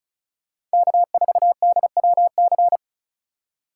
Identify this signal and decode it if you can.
Morse code, heard as a keyed tone, about 36 wpm, 700 Hz. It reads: K4DWC
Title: USA Call Signs - 5 Characters - 2x Character Spacing - 35wpm